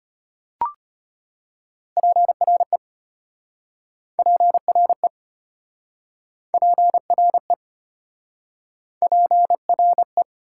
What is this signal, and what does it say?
Morse code, heard as a keyed tone, about 33 wpm, 690 Hz. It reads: PRE PRE PRE PRE